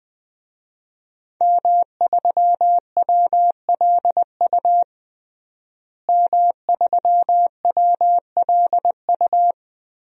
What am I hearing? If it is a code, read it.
Morse code, heard as a keyed tone, about 20 wpm, 700 Hz. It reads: M3WLU M3WLU
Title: DXCC-100 Call Signs - 5 Characters - Repeated 3x - 20wpm